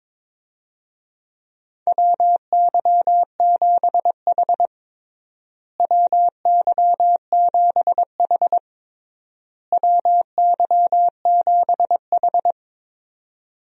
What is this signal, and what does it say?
Morse code, heard as a keyed tone, about 21 wpm, 700 Hz. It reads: WY7H WY7H WY7H
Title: USA Call Signs - 4 Characters - Repeated 3x - 22wpm